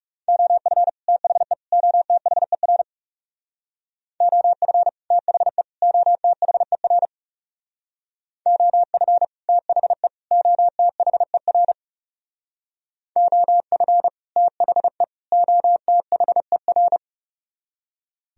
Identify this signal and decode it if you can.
Morse code, heard as a keyed tone, about 39 wpm, 700 Hz. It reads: OF THE OTHER OF THE OTHER OF THE OTHER OF THE OTHER